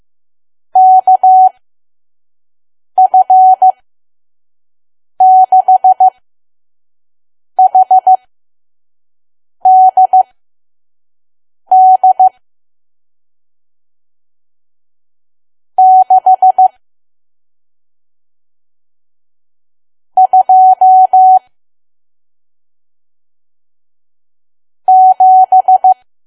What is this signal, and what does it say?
Morse code, heard as a keyed tone, about 15 wpm, spaced as for 5 wpm, 750 Hz. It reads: KF6HDD 6 2 7
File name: Archive / 5 WPM files / PM